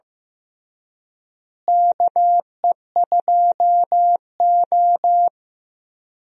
Morse code, heard as a keyed tone, about 15 wpm, 700 Hz. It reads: KE2O
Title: USA Call Signs - 4 Characters - Repeated 2x - 15wpm